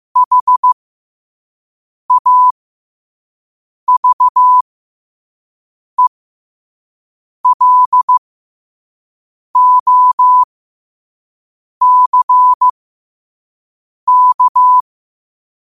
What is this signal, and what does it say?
Morse code, heard as a keyed tone, about 15 wpm, spaced as for 5 wpm, 1000 Hz. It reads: HAVELOCK
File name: qotdfiles / 5wpm